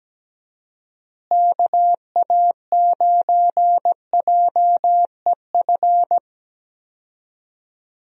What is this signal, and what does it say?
Morse code, heard as a keyed tone, about 17 wpm, 700 Hz. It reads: KA9JEF
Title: USA Call Signs - 6 Characters - Repeated 2x - 17wpm